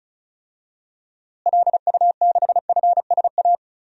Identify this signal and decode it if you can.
Morse code, heard as a keyed tone, about 36 wpm, 700 Hz. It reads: LU6FSA
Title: DXCC-100 Call Signs - 6 Characters - Repeated 3x - 35wpm